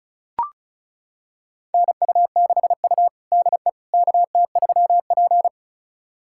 Morse code, heard as a keyed tone, about 36 wpm, 700 Hz. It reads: NU6U DE KT3P
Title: USA Callsign DE Callsign - 4 Characters - 35wpm